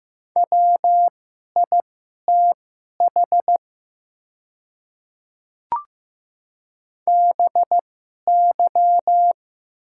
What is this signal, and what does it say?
Morse code, heard as a keyed tone, about 15 wpm, spaced as for 11 wpm, 700 Hz. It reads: WITH BY
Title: 2x Character Spacing - Top 100 Words - 15wpm